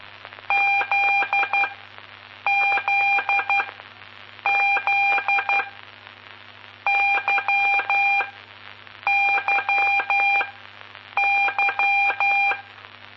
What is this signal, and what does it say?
Morse code, heard as a keyed tone, about 12 wpm, spaced as for 7 wpm, 2400 Hz, with noise. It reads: ZZZ YYY
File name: pp5bar / aulas de telegrafia / Aula-11